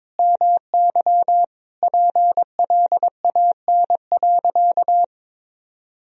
Morse code, heard as a keyed tone, about 22 wpm, 700 Hz. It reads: MY PLAN.
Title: Sentences from Top 500 Words and Abbreviations 22wpm